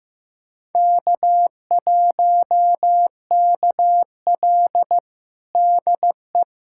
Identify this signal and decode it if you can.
Morse code, heard as a keyed tone, about 15 wpm, 700 Hz. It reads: K1KL DE
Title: USA Callsign DE Callsign - 4 Characters - 15wpm